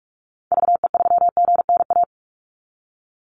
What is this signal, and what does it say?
Morse code, heard as a keyed tone, about 44 wpm, 700 Hz. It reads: VE3GNA
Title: Call Signs - 45wpm